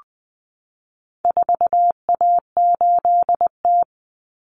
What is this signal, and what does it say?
Morse code, heard as a keyed tone, about 20 wpm, 700 Hz. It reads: 4A8T
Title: Common Cut Numbers - 4-Digit 20wpm